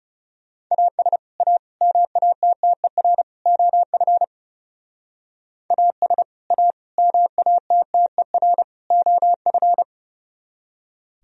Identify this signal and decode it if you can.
Morse code, heard as a keyed tone, about 35 wpm, 700 Hz. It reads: AS A MATTER OF AS A MATTER OF
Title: Speed-Racing - Sets of 4 Words 30wpm